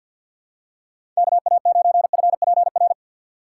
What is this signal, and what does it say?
Morse code, heard as a keyed tone, about 48 wpm, 700 Hz. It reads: K A 9 F P R
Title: USA Call Signs - 6 Characters - 6x Character Spacing - 50wpm